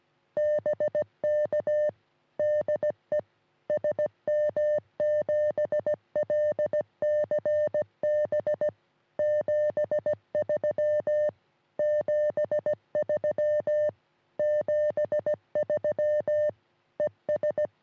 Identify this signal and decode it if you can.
Morse code, heard as a keyed tone, about 17 wpm, 600 Hz. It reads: BK DE SM7LCB 73 73 73 ES
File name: sm7lcb / sm3lbn 3cm rs